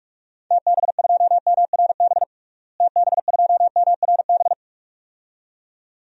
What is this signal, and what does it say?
Morse code, heard as a keyed tone, about 42 wpm, 700 Hz. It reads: TB2MRB TB2MRB
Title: DXCC-100 Call Signs - 6 Characters - Repeated 2x - 45wpm